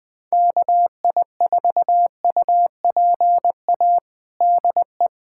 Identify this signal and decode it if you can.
Morse code, heard as a keyed tone, about 20 wpm, 700 Hz. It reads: KI4UPA DE
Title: USA Callsign DE Callsign 20wpm